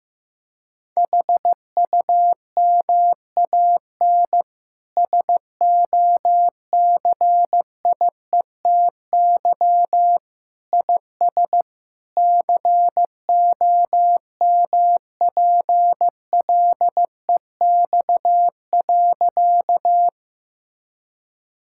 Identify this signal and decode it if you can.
Morse code, heard as a keyed tone, about 15 wpm, 700 Hz. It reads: HUMAN SOCIETY IS COMPLEX.
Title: Sentences from Top 1000 Words - Review Part 6 - 15wpm